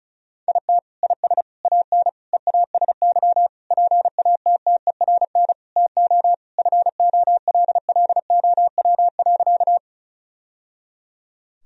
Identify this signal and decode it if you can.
Morse code, heard as a keyed tone, about 36 wpm, 700 Hz. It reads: IT IS AN EASY PATTERN TO FOLLOW.
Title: Sentences from Top 1000 Words - Review Part 2 - 35wpm